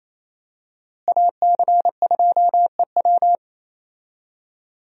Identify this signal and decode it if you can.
Morse code, heard as a keyed tone, about 27 wpm, 700 Hz. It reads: AC2EW
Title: USA Call Signs - 5 Characters - Repeated 3x - 28wpm